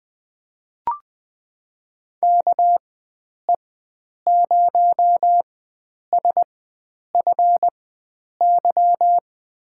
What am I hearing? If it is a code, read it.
Morse code, heard as a keyed tone, about 20 wpm, spaced as for 9 wpm, 700 Hz. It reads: KE0SFY
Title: USA Call Signs - 6 Characters - 4x Character Spacing - 20wpm